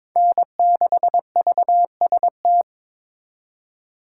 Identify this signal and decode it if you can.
Morse code, heard as a keyed tone, about 26 wpm, 700 Hz. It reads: N6VST
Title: Speed Racing - USA 1x3 Call Signs 22wpm